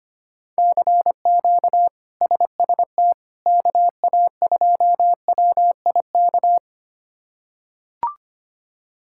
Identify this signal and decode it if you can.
Morse code, heard as a keyed tone, about 25 wpm, 700 Hz. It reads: CQ SST KA2WIK
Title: K1USN SST QSO Practice - 25wpm